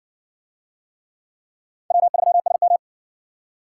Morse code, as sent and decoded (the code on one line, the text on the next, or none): .-- ...-- ... --
W3SM